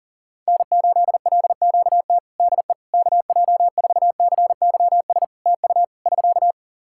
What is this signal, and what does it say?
Morse code, heard as a keyed tone, about 37 wpm, 700 Hz. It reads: N8LQT DE KJ4CYS TU <SK>
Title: K1USN SST QSO Practice 40wpm